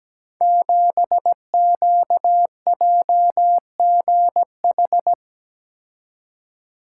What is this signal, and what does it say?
Morse code, heard as a keyed tone, about 17 wpm, 700 Hz. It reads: 7QJGH
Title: Code Groups - 5 Letter:Numbers - 17wpm